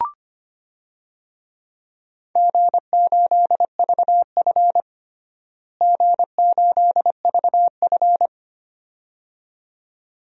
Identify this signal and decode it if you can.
Morse code, heard as a keyed tone, about 25 wpm, 700 Hz. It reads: G8VF G8VF
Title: DXCC-100 Call Signs - 4 Characters - Repeated 2x - 25wpm